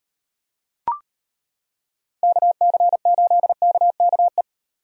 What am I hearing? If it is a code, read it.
Morse code, heard as a keyed tone, about 30 wpm, 700 Hz. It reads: KC8KKE
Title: Speed Racing - USA 2x3 Call Signs 25wpm